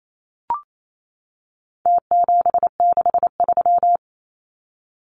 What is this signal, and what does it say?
Morse code, heard as a keyed tone, about 28 wpm, 700 Hz. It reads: T763
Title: Numbers - Contest Style - 4-Digit 28wpm